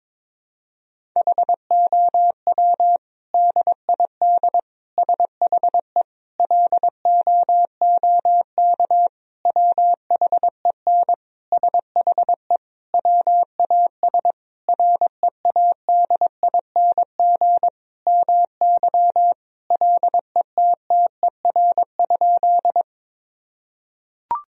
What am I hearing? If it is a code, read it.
Morse code, heard as a keyed tone, about 21 wpm, 700 Hz. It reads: HOW DID SHE LOOK WHEN SHE WAS READING MY LETTER?
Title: Sentences from Top 1000 Words - Review Part 3 - 22wpm